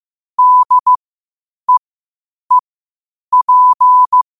Morse code, heard as a keyed tone, about 15 wpm, spaced as for 8 wpm, 1000 Hz. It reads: DEEP
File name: qotdfiles / 8wpm